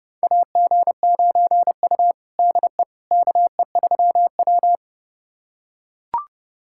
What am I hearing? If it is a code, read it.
Morse code, heard as a keyed tone, about 30 wpm, 700 Hz. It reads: AG9U DE KE3W
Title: USA Callsign DE Callsign - 4 Characters - 30wpm